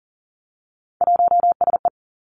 Morse code, heard as a keyed tone, about 40 wpm, 700 Hz. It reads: 1SE
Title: Three Challenging Letter-Number Combinations 40wpm